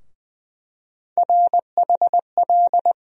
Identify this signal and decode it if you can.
Morse code, heard as a keyed tone, about 21 wpm, 700 Hz. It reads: RHL
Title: USA Call Sign Postfixes - 3 Letters 20wpm